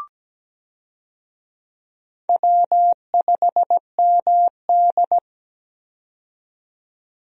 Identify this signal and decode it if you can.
Morse code, heard as a keyed tone, about 17 wpm, 700 Hz. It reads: W5MD
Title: USA Call Signs - 4 Characters - 17wpm